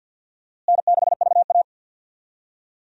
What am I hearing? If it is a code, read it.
Morse code, heard as a keyed tone, about 52 wpm, 700 Hz. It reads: N6VA
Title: USA Call Signs - 4 Characters - Repeated 3x - 50wpm